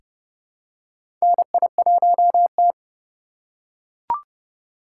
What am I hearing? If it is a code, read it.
Morse code, heard as a keyed tone, about 31 wpm, 700 Hz. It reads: NI1T
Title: USA Call Signs - 4 Characters - 3x Character Spacing - 30wpm